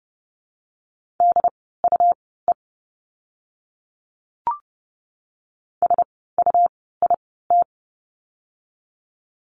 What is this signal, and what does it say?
Morse code, heard as a keyed tone, about 30 wpm, spaced as for 17 wpm, 700 Hz. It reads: DUE SUIT
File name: Lesson 17f - Letter u - Words 30wpm